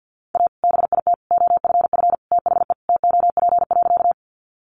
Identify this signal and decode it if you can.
Morse code, heard as a keyed tone, about 49 wpm, 700 Hz. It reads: A BIT OFF THE TOP.
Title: Sentences from Top 900 Words 50wpm